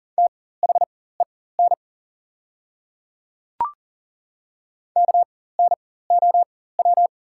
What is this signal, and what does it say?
Morse code, heard as a keyed tone, about 40 wpm, spaced as for 19 wpm, 690 Hz. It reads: THEN KNOW
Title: 4x Character Spacing - Top 100 Words - 40wpm